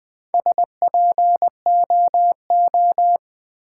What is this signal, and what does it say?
Morse code, heard as a keyed tone, about 20 wpm, 700 Hz. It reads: SPOO
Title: Build A Word - 1 Letter At A Time - 5-Letter Words - 20wpm